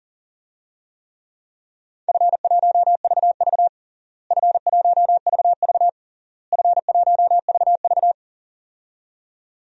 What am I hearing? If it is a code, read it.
Morse code, heard as a keyed tone, about 40 wpm, 700 Hz. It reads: F1VV F1VV F1VV
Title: DXCC-100 Call Signs - 4 Characters - Repeated 3x - 40wpm